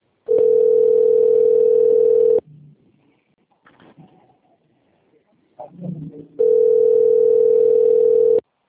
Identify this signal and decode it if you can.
Ringback tone (North America), 440 + 480 Hz.